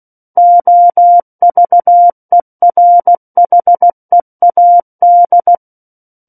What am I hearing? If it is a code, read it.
Morse code, heard as a keyed tone, about 16 wpm, 700 Hz. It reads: OVERHEAD
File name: Book One / 07-Chapter 7 - 16 WPM